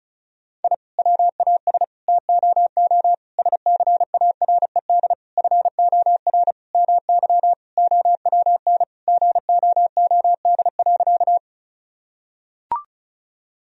Morse code, heard as a keyed tone, about 36 wpm, 700 Hz. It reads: I WAS TOO SCARED FOR MY OWN GOOD.
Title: Sentences from Top 1000 Words - Review Part 4 - 35wpm